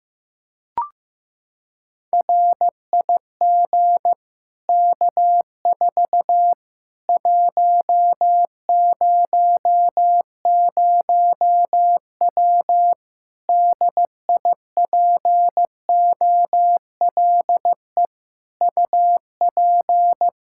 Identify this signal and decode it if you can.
Morse code, heard as a keyed tone, about 15 wpm, 700 Hz. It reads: RIG K4 100W DIPOLE UP
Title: Practice QSO - Exchange 2 - Info - 15wpm